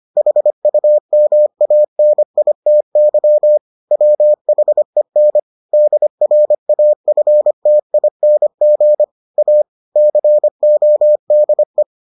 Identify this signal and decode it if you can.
Morse code, heard as a keyed tone, about 25 wpm, 600 Hz. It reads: HUMANITY WHEN DRAFTING A CODE